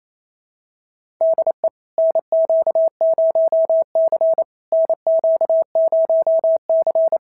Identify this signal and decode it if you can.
Morse code, heard as a keyed tone, about 31 wpm, 650 Hz. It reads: DE NQ0C NQ0C
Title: SOTA QSO Practice 28wpm